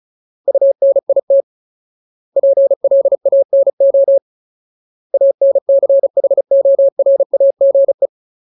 Morse code, heard as a keyed tone, about 34 wpm, 550 Hz. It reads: UNIT PLANO ANCHORAGE
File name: Ultimate Word List / 35WPM Ultimate